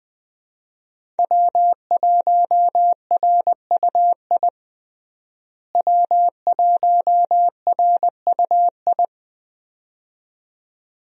Morse code, heard as a keyed tone, about 20 wpm, 700 Hz. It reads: W1RUI W1RUI
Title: USA Call Signs - 5 Characters - Repeated 2x - 20wpm